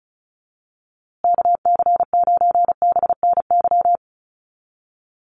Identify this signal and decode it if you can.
Morse code, heard as a keyed tone, about 34 wpm, 700 Hz. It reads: KC9BNY
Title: USA Call Signs Repeated 2x 35wpm